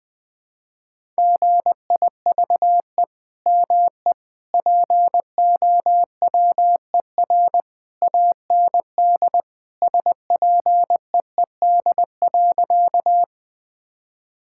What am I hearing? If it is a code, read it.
Morse code, heard as a keyed tone, about 20 wpm, 700 Hz. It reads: GIVE ME POWER AND SPEED.
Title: Sentences from Top 1000 Words - Review Part 5 - 20wpm